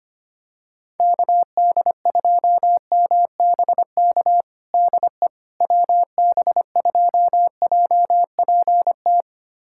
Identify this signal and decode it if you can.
Morse code, heard as a keyed tone, about 25 wpm, 700 Hz. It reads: KD2MBK DE WB2JPT
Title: USA Callsign DE Callsign 25wpm